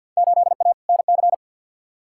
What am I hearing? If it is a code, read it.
Morse code, heard as a keyed tone, about 49 wpm, 700 Hz. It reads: 8A NC
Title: Field Day Exchanges - 50wpm